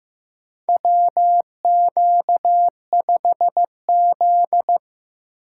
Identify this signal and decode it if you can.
Morse code, heard as a keyed tone, about 15 wpm, 700 Hz. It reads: WQ5Z
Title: USA Call Signs - 4 Characters - 2x Character Spacing - 15wpm